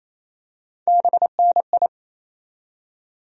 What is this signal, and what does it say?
Morse code, heard as a keyed tone, about 28 wpm, 700 Hz. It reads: BNI
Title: USA Call Sign Postfixes - 3 Letters 28wpm